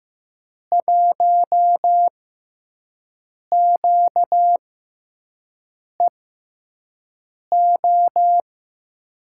Morse code, heard as a keyed tone, about 15 wpm, spaced as for 5 wpm, 700 Hz. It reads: 1QEO
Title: USA Call Signs - 6 Characters - 6x Character Spacing - 15wpm